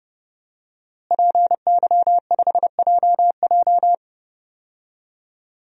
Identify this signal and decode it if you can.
Morse code, heard as a keyed tone, about 32 wpm, 700 Hz. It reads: PY5JJ
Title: DXCC-100 Call Signs - 5 Characters - Repeated 2x - 30wpm